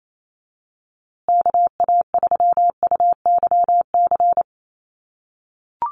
Morse code, heard as a keyed tone, about 28 wpm, 700 Hz. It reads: KA3UYC
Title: USA Call Signs Repeated 2x 28wpm